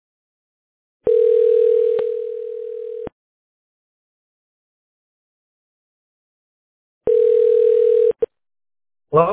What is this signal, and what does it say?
Ringback tone (North America), 440 + 480 Hz.